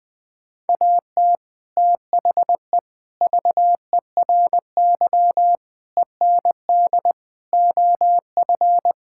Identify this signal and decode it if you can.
Morse code, heard as a keyed tone, about 21 wpm, 700 Hz. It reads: AT THE VERY END OF
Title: Sets of 6 Words - 20wpm